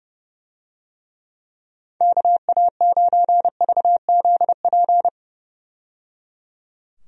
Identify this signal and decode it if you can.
Morse code, heard as a keyed tone, about 30 wpm, 700 Hz. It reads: KA9VZP
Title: USA Call Signs - 6 Characters - 30wpm